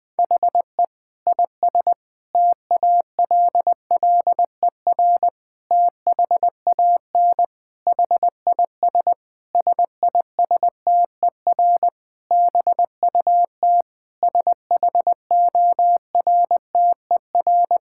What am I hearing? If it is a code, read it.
Morse code, heard as a keyed tone, about 20 wpm, 700 Hz. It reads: HE IS TALLER THAN HIS SISTER BUT SHORTER
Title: Difficult Sentences with Hints - 20wpm